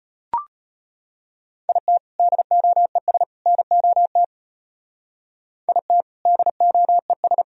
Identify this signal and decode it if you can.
Morse code, heard as a keyed tone, about 30 wpm, 700 Hz. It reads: IT DOES NOT IT DOES
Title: Speed-Racing - Sets of 3 Words 25wpm